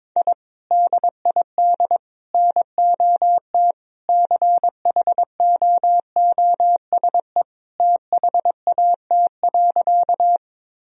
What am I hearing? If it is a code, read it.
Morse code, heard as a keyed tone, about 21 wpm, 700 Hz. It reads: I DID NOT CHOOSE THAT.
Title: Sentences from Top 1000 Words - Review Part 4 - 22wpm